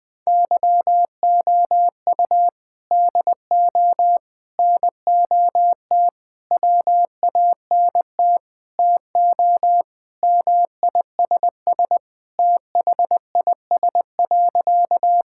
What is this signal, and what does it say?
Morse code, heard as a keyed tone, about 20 wpm, 700 Hz. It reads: YOU DO NOT WANT TO MISS THIS.
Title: Sentences with 4-Word Phrases - 20wpm